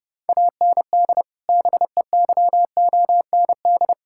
Morse code, heard as a keyed tone, about 30 wpm, 700 Hz. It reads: AND BEYOND
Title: Sentences from Top 1000 Words - Review Part 6 - 30wpm